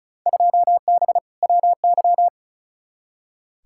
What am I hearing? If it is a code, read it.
Morse code, heard as a keyed tone, about 34 wpm, 700 Hz. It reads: 2B WY
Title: Field Day Exchanges Encore - 35wpm